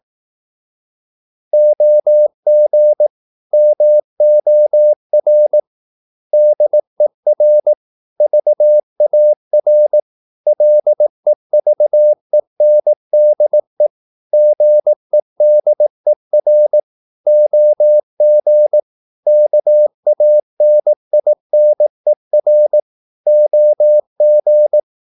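Morse code, heard as a keyed tone, about 18 wpm, 600 Hz. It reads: OG MOR DER VAR LEVENDE GEDER OG KANINER OG
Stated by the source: Slow (QRS) Morse practice at 18 wpm, CW med Gnister.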